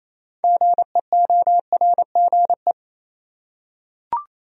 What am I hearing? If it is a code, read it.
Morse code, heard as a keyed tone, about 28 wpm, 700 Hz. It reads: GEORGE
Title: USA Common Names - 28wpm